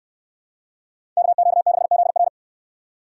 Morse code, heard as a keyed tone, about 69 wpm, 700 Hz. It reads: KQ6ZR